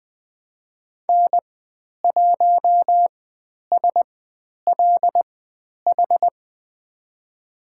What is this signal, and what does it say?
Morse code, heard as a keyed tone, about 20 wpm, spaced as for 10 wpm, 700 Hz. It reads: N1SLH
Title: 095 Lesson 16g - Letter c - Call Signs 20wpm